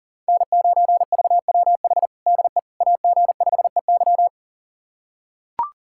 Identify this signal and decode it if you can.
Morse code, heard as a keyed tone, about 37 wpm, 700 Hz. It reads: N9VWH DE AG5EY
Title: USA Callsign DE Callsign - 5 Characters - 40wpm